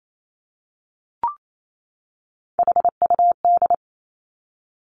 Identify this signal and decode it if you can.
Morse code, heard as a keyed tone, about 28 wpm, 700 Hz. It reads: HUD
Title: Three Challenging Letter-Number Combinations 28wpm